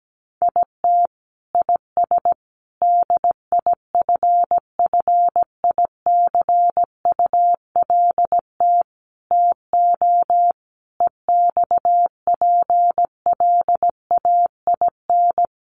Morse code, heard as a keyed tone, about 17 wpm, 700 Hz. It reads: IT IS DIFFICULT TO EXPLAIN
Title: Sentences from Top 1100 Words 17wpm